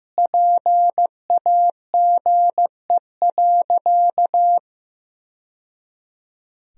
Morse code, heard as a keyed tone, about 15 wpm, 700 Hz. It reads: PAGE.
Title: Sentences from Top 1000 Words - Review Part 1 - 15wpm